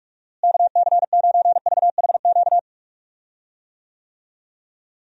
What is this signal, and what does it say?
Morse code, heard as a keyed tone, about 46 wpm, 700 Hz. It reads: KC9VHQ
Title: USA Call Signs - 6 Characters - Repeated 2x - 45wpm